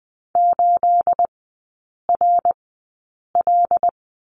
Morse code, heard as a keyed tone, about 20 wpm, spaced as for 8 wpm, 690 Hz. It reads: 8RL
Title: 212 Lesson 34e - Number 8 - Three Characters 20wpm